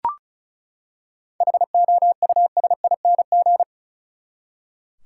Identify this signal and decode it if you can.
Morse code, heard as a keyed tone, about 35 wpm, 700 Hz. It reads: HOUSING
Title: Seven Letter Words - 35wpm